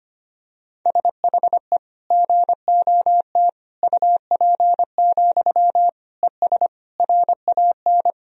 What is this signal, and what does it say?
Morse code, heard as a keyed tone, about 25 wpm, 700 Hz. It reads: SHE GOT UP, ES RAN
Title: Sentences from Top 500 Words and Abbreviations 25wpm